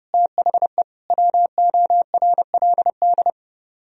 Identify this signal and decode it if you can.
Morse code, heard as a keyed tone, about 36 wpm, 690 Hz. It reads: THE WORLD
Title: Speed-Racing - Sets of 4 Words 30wpm